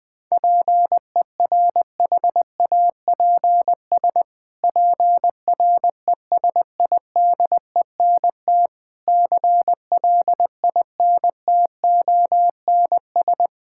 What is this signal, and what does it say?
Morse code, heard as a keyed tone, about 21 wpm, 700 Hz. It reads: PERHAPS PRESIDENT CLINTONS
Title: Pangram Challenge v2 - 20wpm